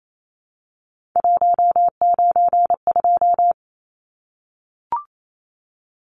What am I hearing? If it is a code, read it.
Morse code, heard as a keyed tone, about 28 wpm, 700 Hz. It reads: 192
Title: Numbers - 3-Digit 28wpm